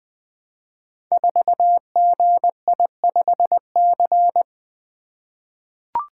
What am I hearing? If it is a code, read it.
Morse code, heard as a keyed tone, about 20 wpm, 700 Hz. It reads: 4GI5C
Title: Code Groups - 5 Letter:Numbers - 20wpm